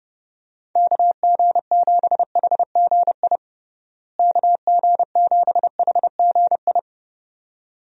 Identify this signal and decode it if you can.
Morse code, heard as a keyed tone, about 30 wpm, 700 Hz. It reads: KG7HGI KG7HGI
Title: USA Call Signs - 6 Characters - Repeated 2x - 30wpm